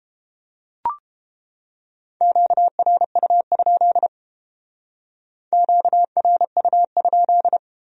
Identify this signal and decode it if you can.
Morse code, heard as a keyed tone, about 30 wpm, 700 Hz. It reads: QRU? QRU?